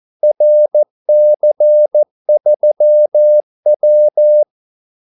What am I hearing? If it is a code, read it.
Morse code, heard as a keyed tone, about 14 wpm, 600 Hz. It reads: RC3W
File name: Callsigns-14wpm